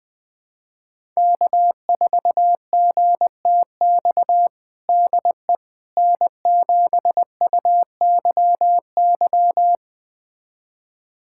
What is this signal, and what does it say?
Morse code, heard as a keyed tone, about 21 wpm, 700 Hz. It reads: K4GTX DE N7UYY
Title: USA Callsign DE Callsign 20wpm